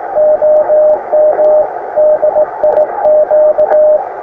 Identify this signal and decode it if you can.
Morse code, heard as a keyed tone, about 15 wpm, 610 Hz, with noise. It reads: OM DIQ
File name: attachments / iz0aym 20m